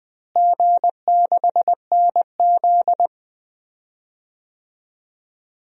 Morse code, heard as a keyed tone, about 20 wpm, 700 Hz. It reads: G6NZ
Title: DXCC-100 Call Signs - 4 Characters - Repeated 2x - 20wpm